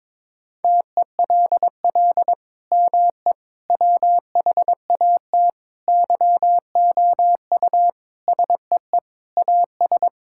Morse code, heard as a keyed tone, about 22 wpm, 700 Hz. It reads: TELL ME WHAT YOU SEE AS
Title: Sentences from Top 1600 Words 22wpm